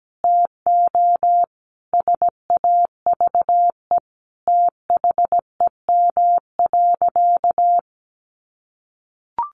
Morse code, heard as a keyed tone, about 17 wpm, 700 Hz. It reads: TO SAVE THEM.
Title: Sentences from Top 1100 Words 17wpm